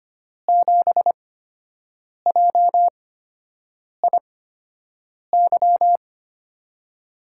Morse code, heard as a keyed tone, about 25 wpm, spaced as for 7 wpm, 700 Hz. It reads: 7JIY
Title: USA Call Signs - 6 Characters - 8x Character Spacing - 25wpm